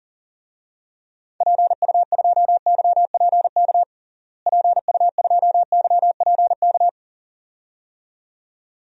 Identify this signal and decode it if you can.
Morse code, heard as a keyed tone, about 37 wpm, 700 Hz. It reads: PU2YPK PU2YPK
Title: DXCC-100 Call Signs - 6 Characters - Repeated 2x - 40wpm